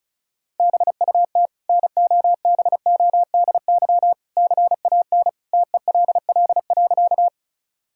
Morse code, heard as a keyed tone, about 36 wpm, 700 Hz. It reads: BUT NOBODY CAN TELL.
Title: Sentences from Top 1600 Words 35wpm